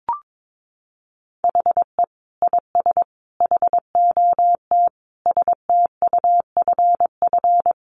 Morse code, heard as a keyed tone, about 22 wpm, 700 Hz. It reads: HE IS HOT STUFF
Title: Sentences from Top 1500 Words 22wpm